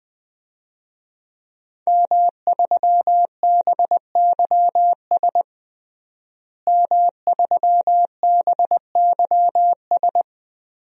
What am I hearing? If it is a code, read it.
Morse code, heard as a keyed tone, about 21 wpm, 700 Hz. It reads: M3BYS M3BYS
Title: DXCC-100 Call Signs - 5 Characters - Repeated 2x - 20wpm